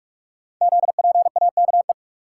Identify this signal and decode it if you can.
Morse code, heard as a keyed tone, about 44 wpm, 690 Hz. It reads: ZPAKE
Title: Code Groups - 5 Letter:Numbers - 45wpm